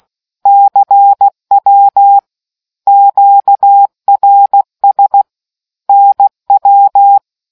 Morse code, heard as a keyed tone, about 16 wpm, 800 Hz. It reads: CW QRS NW